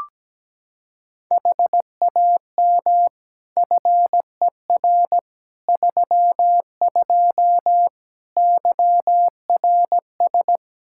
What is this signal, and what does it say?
Morse code, heard as a keyed tone, about 17 wpm, 700 Hz. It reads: HAM FER 32 YRS
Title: K1USN SST QSO Practice 17wpm